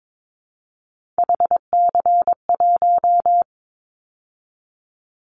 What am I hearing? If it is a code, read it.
Morse code, heard as a keyed tone, about 22 wpm, 700 Hz. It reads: HC1
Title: Callsign Prefixes for Top 100 DXCC - 22wpm